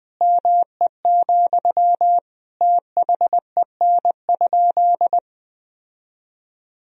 Morse code, heard as a keyed tone, about 20 wpm, 700 Hz. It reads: ME, THEN?
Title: Sentences from Top 1000 Words - Review Part 3 - 20wpm